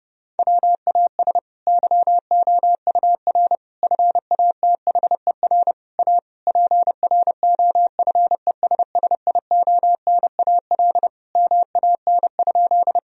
Morse code, heard as a keyed tone, about 30 wpm, 700 Hz. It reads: WAS YOUR FATHER A PROFESSIONAL MAN?
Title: Sentences from Top 1000 Words - Review Part 4 - 30wpm